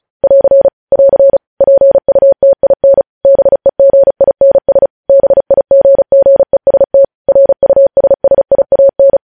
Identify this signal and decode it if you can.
Morse code, heard as a keyed tone, about 36 wpm, 550 Hz. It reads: <AR> <AR> PUTIN BEGINS BIGGEST RUSSIAN
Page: news headlines in morse code at 35 wpm